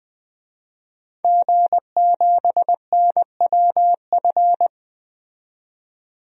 Morse code, heard as a keyed tone, about 20 wpm, 700 Hz. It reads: G7NWF
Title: DXCC-100 Call Signs - 5 Characters - 20wpm